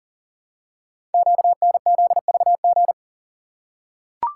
Morse code, heard as a keyed tone, about 42 wpm, 700 Hz. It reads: QNZVG
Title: Code Groups - 5 Letters - 40wpm